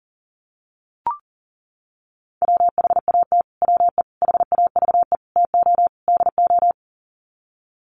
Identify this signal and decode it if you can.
Morse code, heard as a keyed tone, about 40 wpm, 700 Hz. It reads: WHAT WE HAVE TO DO
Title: Sets of 5 Words Encore 40wpm